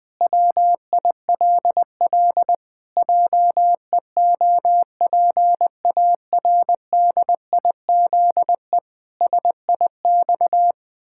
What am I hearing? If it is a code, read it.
Morse code, heard as a keyed tone, about 20 wpm, 700 Hz. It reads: WILL JEOPARDIZE SIX
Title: Pangram Challenge v2 - 20wpm